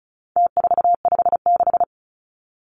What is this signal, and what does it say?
Morse code, heard as a keyed tone, about 34 wpm, 700 Hz. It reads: T456
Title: Numbers - Contest Style - 4-Digit 35wpm